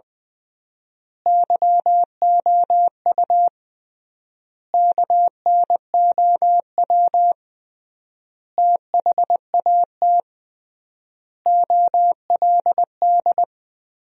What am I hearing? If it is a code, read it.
Morse code, heard as a keyed tone, about 20 wpm, 700 Hz. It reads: YOU KNOW THAT OLD
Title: 3x Word-Spacing - Sentences from Top 400 Words - 20wpm